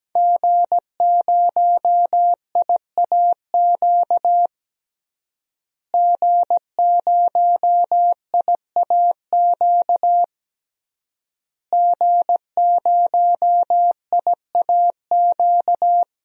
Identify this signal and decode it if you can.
Morse code, heard as a keyed tone, about 17 wpm, 700 Hz. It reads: G0IAQ G0IAQ G0IAQ
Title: DXCC-100 Call Signs - 5 Characters - Repeated 3x - 17wpm